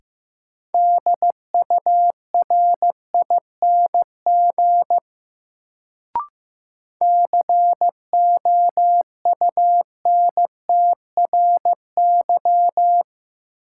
Morse code, heard as a keyed tone, about 15 wpm, 700 Hz. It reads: DURING COUNTRY
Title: Sets of 2 Words - New Word Drill - 15wpm